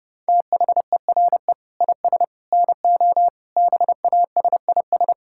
Morse code, heard as a keyed tone, about 32 wpm, 700 Hz. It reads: THERE IS NO BASIS
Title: Sentences from Top 1000 Words - Review Part 5 - 30wpm